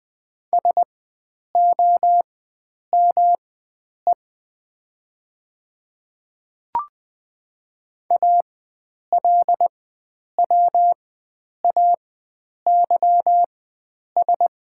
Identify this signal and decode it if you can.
Morse code, heard as a keyed tone, about 20 wpm, spaced as for 9 wpm, 700 Hz. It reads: SOME ALWAYS
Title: 4x Character Spacing - Top 300 Words - 20wpm